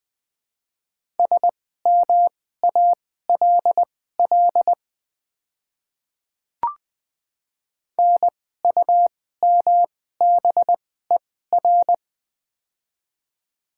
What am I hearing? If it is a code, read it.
Morse code, heard as a keyed tone, about 20 wpm, spaced as for 14 wpm, 700 Hz. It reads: SMALL NUMBER
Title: 2x Character Spacing - Top 200 Words - 20wpm